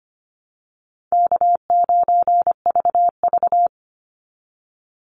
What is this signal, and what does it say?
Morse code, heard as a keyed tone, about 25 wpm, 700 Hz. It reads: K9VV
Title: Call Signs - 25wpm